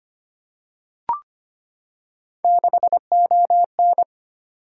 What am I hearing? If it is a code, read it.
Morse code, heard as a keyed tone, about 25 wpm, 700 Hz. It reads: 6ON
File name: Three Letter-Number Combinations - 25wpm